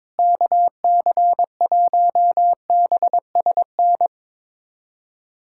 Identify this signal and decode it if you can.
Morse code, heard as a keyed tone, about 22 wpm, 700 Hz. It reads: KC1BSN
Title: USA Callsign DE Callsign 22wpm